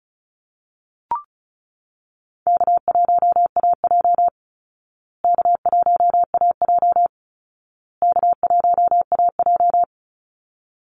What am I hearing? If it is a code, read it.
Morse code, heard as a keyed tone, about 34 wpm, 700 Hz. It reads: K1AJ K1AJ K1AJ
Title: Easy Call Signs Repeated 3x - 35wpm